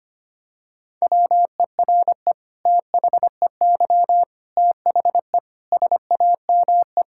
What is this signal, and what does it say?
Morse code, heard as a keyed tone, about 25 wpm, 700 Hz. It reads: WERE THEY THE SAME
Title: Sentences from Top 1600 Words 25wpm